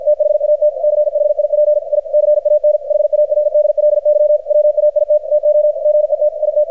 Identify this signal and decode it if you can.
Morse code, heard as a keyed tone, about 50 wpm, 600 Hz, with radio noise. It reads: THAT CHIP I CAN HARDLY WAIT TO GET IT
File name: archived / CRSnet